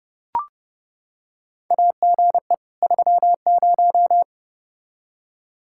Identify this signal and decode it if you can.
Morse code, heard as a keyed tone, about 32 wpm, 700 Hz. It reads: AGE 30
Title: Practice QSO - Exchange 3 - More Info - 30wpm